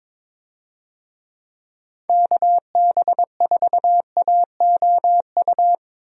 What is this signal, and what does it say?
Morse code, heard as a keyed tone, about 22 wpm, 700 Hz. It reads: KB4AOU
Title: USA Call Signs - 6 Characters - Repeated 3x - 22wpm